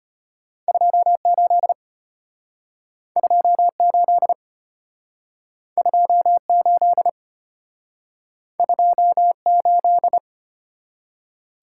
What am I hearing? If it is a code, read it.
Morse code, heard as a keyed tone, about 34 wpm, 700 Hz. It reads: 28 28 28 28